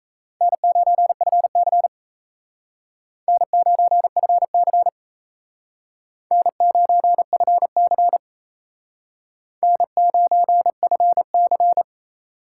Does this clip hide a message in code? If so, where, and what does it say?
Yes: Morse code from 0.4 to 11.8 s, heard as a keyed tone, about 37 wpm, 700 Hz. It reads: N9FC N9FC N9FC N9FC